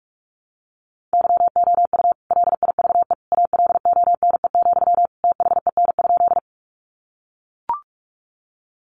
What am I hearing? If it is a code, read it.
Morse code, heard as a keyed tone, about 44 wpm, 700 Hz. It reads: YOU LIVE ALONE, THEN?
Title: Sentences from Top 1000 Words 45wpm